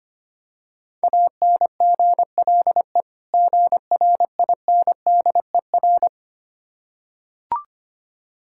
Morse code, heard as a keyed tone, about 25 wpm, 690 Hz. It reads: ANGLE GRINDER
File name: practice / 25wpm / Difficult Words with Hints - 25wpm